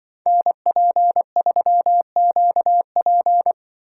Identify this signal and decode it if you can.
Morse code, heard as a keyed tone, about 24 wpm, 700 Hz. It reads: NP3QP
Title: Speed Racing - USA 2x2 Call Signs 20wpm